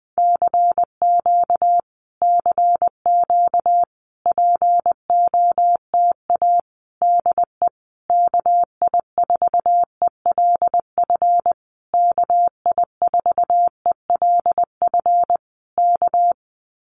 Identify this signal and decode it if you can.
Morse code, heard as a keyed tone, about 20 wpm, 700 Hz. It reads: CQ CQ POTA DE KI4ELF KI4ELF K
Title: Common QSO Phrases Encore 20wpm